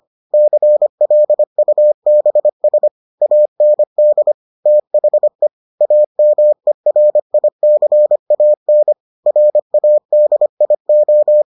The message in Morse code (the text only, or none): CLUBS AND THE AMERICAN RADIO